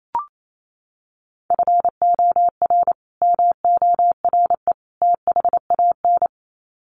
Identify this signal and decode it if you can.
Morse code, heard as a keyed tone, about 28 wpm, 700 Hz. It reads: FOR MORE THAN
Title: Sets of 3 Words 28wpm